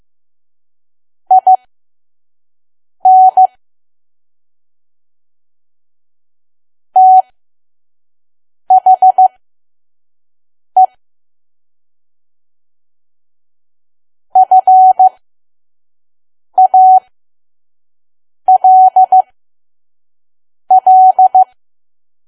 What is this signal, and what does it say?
Morse code, heard as a keyed tone, about 15 wpm, spaced as for 5 wpm, 750 Hz. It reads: IN THE FALL
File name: Archive / 5 WPM files / PM